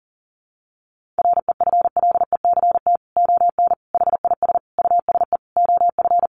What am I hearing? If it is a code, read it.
Morse code, heard as a keyed tone, about 40 wpm, 700 Hz. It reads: REFLECT ON HIS USE OF
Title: Sentences from Top 1500 Words 40wpm